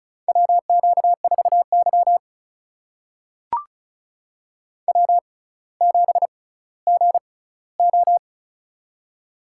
Morse code, heard as a keyed tone, about 36 wpm, spaced as for 18 wpm, 700 Hz. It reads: WQ4Y W7GO
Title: USA Call Signs - 4 Characters - 6x Character Spacing - 35wpm